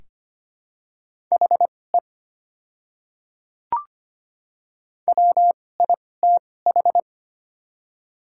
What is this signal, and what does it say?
Morse code, heard as a keyed tone, about 25 wpm, spaced as for 18 wpm, 700 Hz. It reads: HE WITH
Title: 2x Character Spacing - Top 100 Words - 25wpm